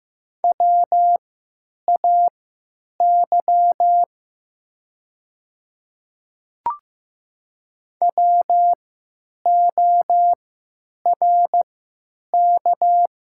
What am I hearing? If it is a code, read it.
Morse code, heard as a keyed tone, about 15 wpm, spaced as for 9 wpm, 700 Hz. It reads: WAY WORK
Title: 3x Character Spacing - Top 200 Words - 15wpm